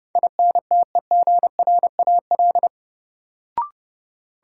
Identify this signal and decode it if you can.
Morse code, heard as a keyed tone, about 30 wpm, 700 Hz. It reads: INTEGRAL
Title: Sets of 5 Words - New Word Drill - 30wpm